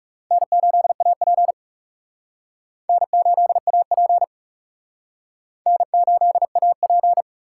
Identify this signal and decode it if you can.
Morse code, heard as a keyed tone, about 35 wpm, 700 Hz. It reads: N8AP N8AP N8AP
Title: Speed Racing - USA 1x2 Call Signs 30wpm